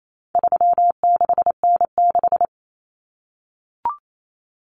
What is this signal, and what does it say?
Morse code, heard as a keyed tone, about 28 wpm, 700 Hz. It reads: 36N6
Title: Common Cut Numbers - 4-Digit 28wpm